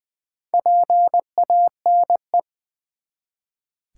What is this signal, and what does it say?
Morse code, heard as a keyed tone, about 20 wpm, 700 Hz. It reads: PANE
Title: Build A Word - 1 Letter At A Time - 5-Letter Words - 20wpm